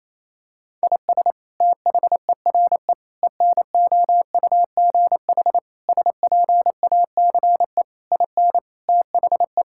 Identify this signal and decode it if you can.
Morse code, heard as a keyed tone, about 28 wpm, 700 Hz. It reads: IS THERE ENOUGH SPACE IN THE
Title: Sentences from Top 1000 Words - Review Part 5 - 28wpm